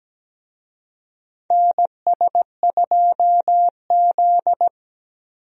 Morse code, heard as a keyed tone, about 17 wpm, 700 Hz. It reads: NS2Z
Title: USA Call Signs - 4 Characters - Repeated 2x - 17wpm